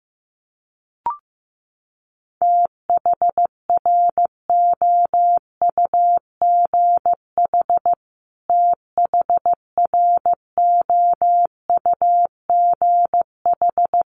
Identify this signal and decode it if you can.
Morse code, heard as a keyed tone, about 14 wpm, 700 Hz. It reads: THROUGH THROUGH
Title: Top 500 Words Repeated 15wpm